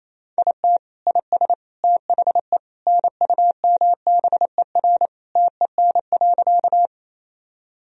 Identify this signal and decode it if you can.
Morse code, heard as a keyed tone, about 27 wpm, 700 Hz. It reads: IT IS THE NUMBER TEN.
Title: Sentences from Top 1000 Words - Review Part 2 - 28wpm